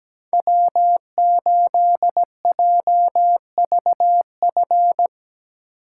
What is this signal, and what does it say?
Morse code, heard as a keyed tone, about 19 wpm, 700 Hz. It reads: W8JVF
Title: Speed Racing - USA 1x3 Call Signs 17wpm